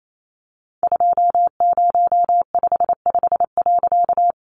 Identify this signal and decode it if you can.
Morse code, heard as a keyed tone, about 28 wpm, 700 Hz. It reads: 2055.
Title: A Sound Of Thunder 28wpm